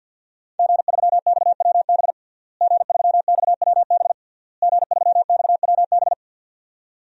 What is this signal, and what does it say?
Morse code, heard as a keyed tone, about 52 wpm, 700 Hz. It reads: G3XWB G3XWB G3XWB
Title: DXCC-100 Call Signs - 5 Characters - Repeated 3x - 50wpm